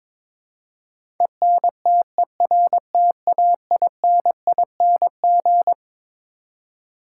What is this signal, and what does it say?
Morse code, heard as a keyed tone, about 22 wpm, 700 Hz. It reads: ENTERTAINING
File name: Twelve Letter Words - 22wpm